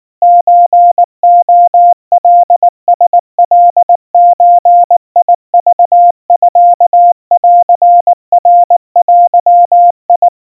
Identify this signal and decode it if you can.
Morse code (Japanese Wabun code), heard as a keyed tone, about 19 wpm, 700 Hz. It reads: ソレカラカゾクミンナデ
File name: kawa / Morse sample / kamyu 70 cpm 700 Hz Crispy